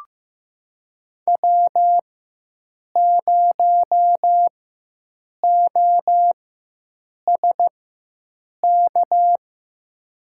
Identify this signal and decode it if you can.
Morse code, heard as a keyed tone, about 15 wpm, spaced as for 7 wpm, 700 Hz. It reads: W0OSK
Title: USA Call Signs - 5 Characters - 4x Character Spacing - 15wpm